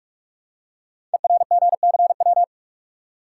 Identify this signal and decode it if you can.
Morse code, heard as a keyed tone, about 41 wpm, 700 Hz. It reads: ERGCW
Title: Code Groups - 5 Letter:Numbers - 45wpm